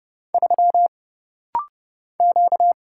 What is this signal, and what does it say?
Morse code, heard as a keyed tone, about 30 wpm, 700 Hz. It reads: 3 Q
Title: Single Letter-Number - Rapid-Fire 30wpm